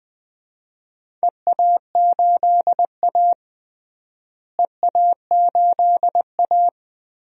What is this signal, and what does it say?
Morse code, heard as a keyed tone, about 20 wpm, 700 Hz. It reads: EA8A EA8A
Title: DXCC-100 Call Signs - 4 Characters - Repeated 3x - 20wpm